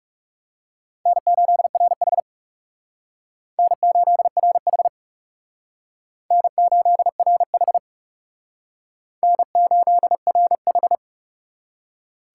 Morse code, heard as a keyed tone, about 37 wpm, 700 Hz. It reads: N8RH N8RH N8RH N8RH
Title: Speed Racing - USA Call Signs 30wpm